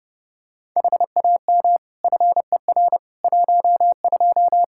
Morse code, heard as a keyed tone, about 30 wpm, 700 Hz. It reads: HAM FER 12
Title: Practice QSO - Exchange 3 - More Info - 30wpm